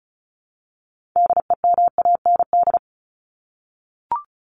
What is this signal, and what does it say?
Morse code, heard as a keyed tone, about 34 wpm, 700 Hz. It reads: DEMAND
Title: Top 501-750 Words 35wpm